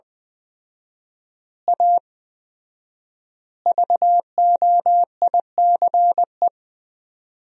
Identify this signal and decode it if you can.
Morse code, heard as a keyed tone, about 21 wpm, 700 Hz. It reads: A VOICE
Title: 4x Word-Spacing - Sentences from Top 500 Words - 20wpm